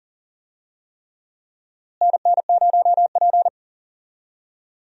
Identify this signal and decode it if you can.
Morse code, heard as a keyed tone, about 40 wpm, 700 Hz. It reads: NN0P
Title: USA Call Signs - 4 Characters - Repeated 3x - 40wpm